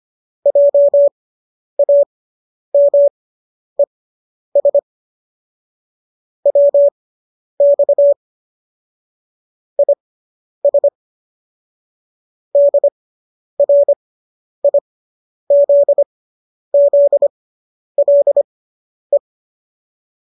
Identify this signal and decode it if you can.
Morse code, heard as a keyed tone, about 25 wpm, spaced as for 10 wpm, 570 Hz. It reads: JAMES WX IS DRIZZLE